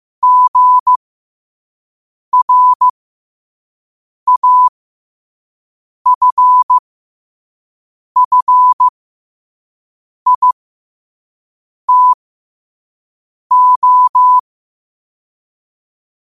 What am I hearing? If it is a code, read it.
Morse code, heard as a keyed tone, about 15 wpm, spaced as for 5 wpm, 1000 Hz. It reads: GRAFFITO
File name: qotdfiles / 5wpm